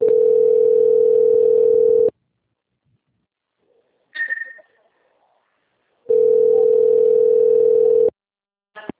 Ringback tone (North America), 440 + 480 Hz.